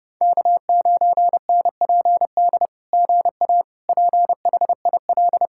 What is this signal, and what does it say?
Morse code, heard as a keyed tone, about 30 wpm, 700 Hz. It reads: K9NPD GA PHIL
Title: SOTA QSO Practice 30wpm